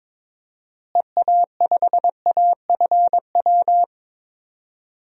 Morse code, heard as a keyed tone, about 22 wpm, 700 Hz. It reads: EA5AFW
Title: DXCC-100 Call Signs - 6 Characters - Repeated 2x - 22wpm